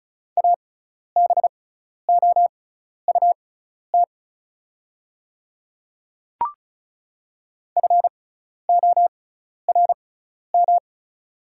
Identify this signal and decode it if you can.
Morse code, heard as a keyed tone, about 36 wpm, spaced as for 12 wpm, 700 Hz. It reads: ABOUT FORM
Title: 6x Character Spacing - Top 300 Words - 35wpm